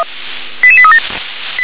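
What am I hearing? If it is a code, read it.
Five-tone selective call (ZVEI), heard as quick runs of five tones, tones of about 70 ms. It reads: 88837
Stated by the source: ZVEI five-tone call received on an SDR.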